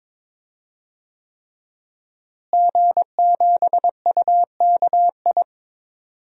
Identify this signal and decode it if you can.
Morse code, heard as a keyed tone, about 22 wpm, 700 Hz. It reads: G7UKI
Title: DXCC-100 Call Signs - 5 Characters - Repeated 3x - 22wpm